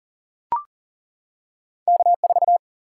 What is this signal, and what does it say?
Morse code, heard as a keyed tone, about 39 wpm, 700 Hz. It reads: K4
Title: Single-Letter Single-Number 40wpm